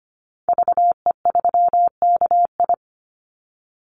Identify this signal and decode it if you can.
Morse code, heard as a keyed tone, about 25 wpm, 700 Hz. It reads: VE3KI
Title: Call Signs - 25wpm